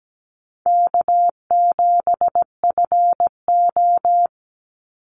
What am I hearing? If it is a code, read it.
Morse code, heard as a keyed tone, about 17 wpm, 700 Hz. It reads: K7FO
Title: NA SOTA Chaser Callsigns 17wpm